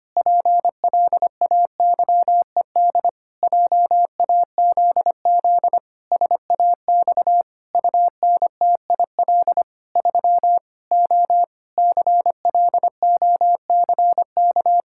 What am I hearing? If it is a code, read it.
Morse code, heard as a keyed tone, about 25 wpm, 700 Hz. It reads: PLAYED JAZZ SAX UNTIL 3 O CLOCK
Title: Pangram Challenge v2 - 25wpm